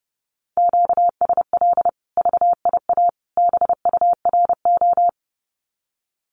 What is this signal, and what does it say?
Morse code, heard as a keyed tone, about 30 wpm, 700 Hz. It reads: QSL VIA BURO
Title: Common QSO Phrases 30wpm